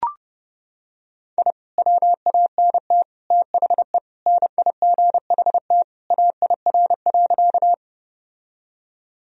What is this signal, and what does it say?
Morse code, heard as a keyed tone, about 32 wpm, 700 Hz. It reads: I WANT THE NIGHT AIR.
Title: Sentences from Top 1000 Words - Review Part 1 - 30wpm